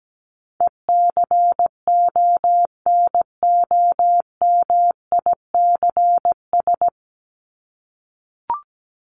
Morse code, heard as a keyed tone, about 17 wpm, 700 Hz. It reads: ECONOMICS
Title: Top 501-750 Words 17wpm